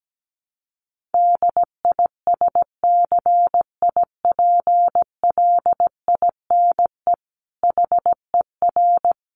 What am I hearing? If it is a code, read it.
Morse code, heard as a keyed tone, about 17 wpm, 700 Hz. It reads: DISCIPLINE HER
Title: Sentences from Top 900 Words 17wpm